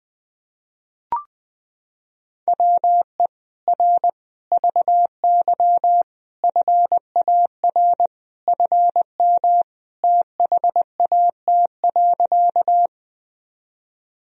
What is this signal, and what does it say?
Morse code, heard as a keyed tone, about 21 wpm, 700 Hz. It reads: WE R VY FAR FM THAT.
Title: Sentences from Top 300 Words and Abbreviations 20wpm